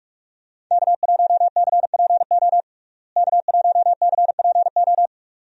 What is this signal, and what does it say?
Morse code, heard as a keyed tone, about 47 wpm, 700 Hz. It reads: K1CPO K1CPO
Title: USA Call Signs - 5 Characters - Repeated 2x - 45wpm